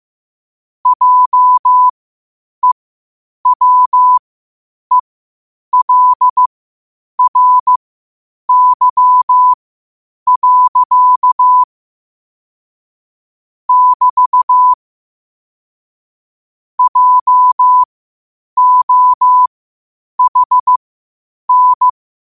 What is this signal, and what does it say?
Morse code, heard as a keyed tone, about 15 wpm, spaced as for 8 wpm, 1000 Hz. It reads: JEWELRY. <BT> JOHN
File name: qotdfiles / 8wpm